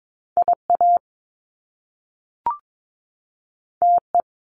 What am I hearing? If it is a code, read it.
Morse code, heard as a keyed tone, about 22 wpm, 700 Hz. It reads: IA TE
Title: Top Two Letter Combinations in Words 22wpm